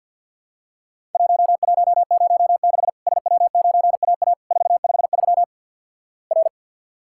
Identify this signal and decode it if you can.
Morse code, heard as a keyed tone, about 50 wpm, 700 Hz. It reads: 1106 SW9AA 453 R
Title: Worked All Europe DX Contest - QTC Exchange - 50wpm